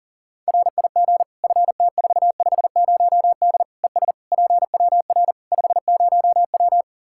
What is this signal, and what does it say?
Morse code, heard as a keyed tone, about 40 wpm, 700 Hz. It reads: RIG FT450D ES PWR 50W
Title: K1USN SST QSO Practice 40wpm